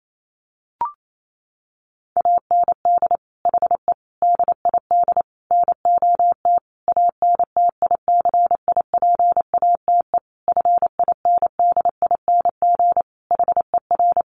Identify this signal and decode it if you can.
Morse code, heard as a keyed tone, about 28 wpm, 700 Hz. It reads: AND HE DID NOT ANTICIPATE FINDING HER
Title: Sentences from Top 1500 Words 28wpm